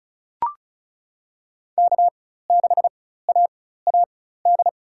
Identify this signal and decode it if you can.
Morse code, heard as a keyed tone, about 36 wpm, spaced as for 16 wpm, 700 Hz. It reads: K6AAD
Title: USA Call Signs - 5 Characters - 4x Character Spacing - 35wpm